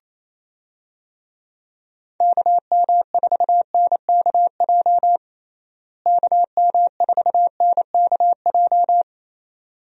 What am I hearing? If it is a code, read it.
Morse code, heard as a keyed tone, about 28 wpm, 700 Hz. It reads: KM4NKJ KM4NKJ
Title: USA Call Signs - 6 Characters - Repeated 2x - 28wpm